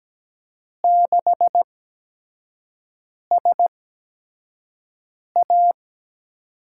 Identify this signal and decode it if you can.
Morse code, heard as a keyed tone, about 17 wpm, spaced as for 5 wpm, 700 Hz. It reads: 6SA
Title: USA Call Signs - 5 Characters - 8x Character Spacing - 17wpm